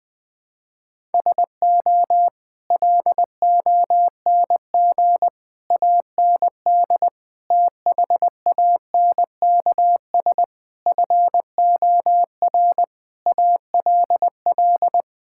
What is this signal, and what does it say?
Morse code, heard as a keyed tone, about 20 wpm, 700 Hz. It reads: SO LONG AND THANKS FOR ALL
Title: Sentences with 8-Word Phrases - 20wpm